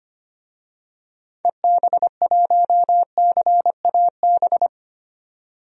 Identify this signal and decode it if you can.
Morse code, heard as a keyed tone, about 25 wpm, 700 Hz. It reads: EB1CAB
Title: DXCC-100 Call Signs - 6 Characters - 25wpm